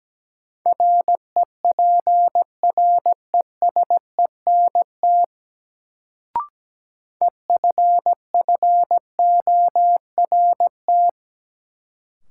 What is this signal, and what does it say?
Morse code, heard as a keyed tone, about 17 wpm, 700 Hz. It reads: REPRESENT EFFORT
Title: Sentences from Top 1200 Words - New Word Drill Alt - 17wpm